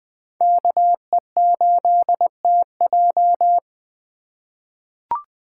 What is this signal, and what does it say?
Morse code, heard as a keyed tone, about 20 wpm, 700 Hz. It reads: KE8TJ
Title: USA Call Signs - 5 Characters - 3x Character Spacing - 20wpm